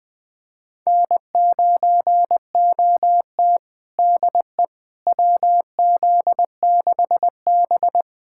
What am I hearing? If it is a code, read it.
Morse code, heard as a keyed tone, about 21 wpm, 700 Hz. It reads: N9OT DE WZ6B
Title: USA Callsign DE Callsign - 4 Characters - 20wpm